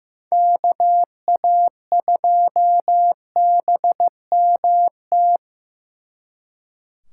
Morse code, heard as a keyed tone, about 18 wpm, 700 Hz. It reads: KA2BMT
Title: Speed Racing - USA 2x3 Call Signs 15wpm